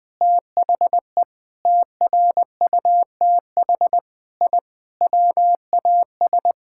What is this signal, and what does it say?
Morse code, heard as a keyed tone, about 20 wpm, 700 Hz. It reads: THE TRUTH I WAS
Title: Sentences with 5-Word Phrases - 20wpm